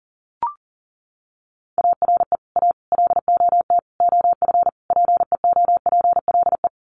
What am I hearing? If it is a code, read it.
Morse code, heard as a keyed tone, about 40 wpm, 700 Hz. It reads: ARE A LOT OF PEOPLE
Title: Sets of 5 Words 40wpm